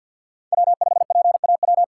Morse code, heard as a keyed tone, about 47 wpm, 700 Hz. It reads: W5PAW
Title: USA Call Signs - 5 Characters - 4x Character Spacing - 50wpm